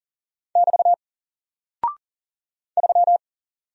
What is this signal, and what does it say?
Morse code, heard as a keyed tone, about 40 wpm, 700 Hz. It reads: <BT> 3
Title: Single Character (Letter:Number:Punctuation:Prosign) Alt 40wpm